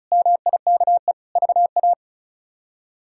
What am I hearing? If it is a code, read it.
Morse code, heard as a keyed tone, about 36 wpm, 700 Hz. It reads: MIKE VA
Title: K1USN SST QSO Practice - 35wpm